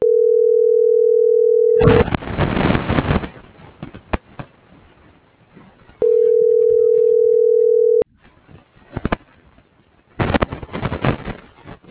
Ringback tone (North America), 440 + 480 Hz.